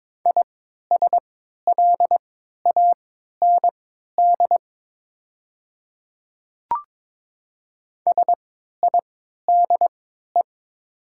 Morse code, heard as a keyed tone, about 22 wpm, spaced as for 12 wpm, 700 Hz. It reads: ISLAND SIDE
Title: 3x Character Spacing - Top 500 Words - 22wpm